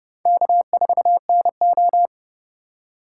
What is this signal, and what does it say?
Morse code, heard as a keyed tone, about 37 wpm, 700 Hz. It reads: K4NO
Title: Speed Racing - USA 1x2 Call Signs 30wpm